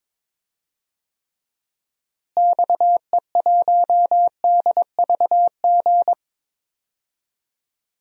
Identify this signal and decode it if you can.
Morse code, heard as a keyed tone, about 22 wpm, 700 Hz. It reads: XE1DVG
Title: DXCC-100 Call Signs - 6 Characters - Repeated 3x - 22wpm